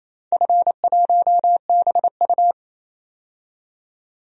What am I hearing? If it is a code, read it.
Morse code, heard as a keyed tone, about 27 wpm, 700 Hz. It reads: F1BU
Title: DXCC-100 Call Signs - 4 Characters - 28wpm